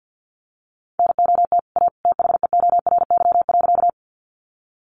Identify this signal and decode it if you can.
Morse code, heard as a keyed tone, about 50 wpm, 700 Hz. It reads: NOT A THEORY.
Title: Sentences from Top 600 Words 50wpm